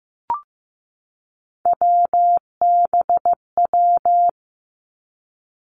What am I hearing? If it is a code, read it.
Morse code, heard as a keyed tone, about 15 wpm, 700 Hz. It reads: WBW
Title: Three Challenging Letter-Number Combinations 15wpm